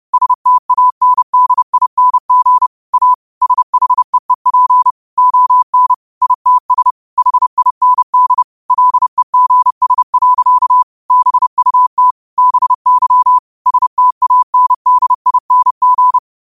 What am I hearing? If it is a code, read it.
Morse code, heard as a keyed tone, about 30 wpm, 1000 Hz. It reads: STANDING A SHEEP ON ITS HIND LEGS. BUT BY STANDING
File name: qotdfiles / 30wpm